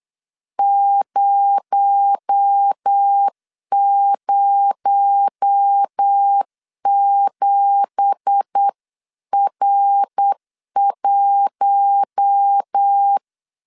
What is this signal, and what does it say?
Morse code, heard as a keyed tone, about 8 wpm, 800 Hz. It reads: 007R1